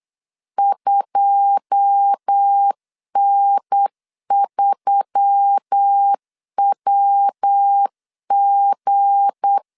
Morse code, heard as a keyed tone, about 8 wpm, 800 Hz. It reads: 2N3WG
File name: LEZIONI / 9 2 8wpm 3437